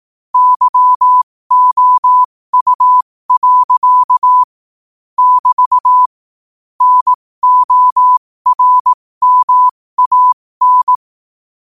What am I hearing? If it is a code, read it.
Morse code, heard as a keyed tone, about 18 wpm, spaced as for 15 wpm, 1000 Hz. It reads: YOU. <BT> NORMAN
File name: qotdfiles / 15wpm